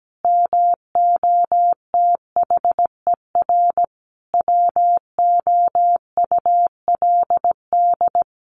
Morse code, heard as a keyed tone, about 17 wpm, 700 Hz. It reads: MOTHER WOULD
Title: Sentences from Top 700 Words 17wpm